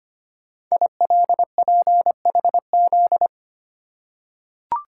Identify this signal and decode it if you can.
Morse code, heard as a keyed tone, about 25 wpm, 700 Hz. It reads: ILPHZ
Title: Code Groups - 5 Letter:Numbers - 25wpm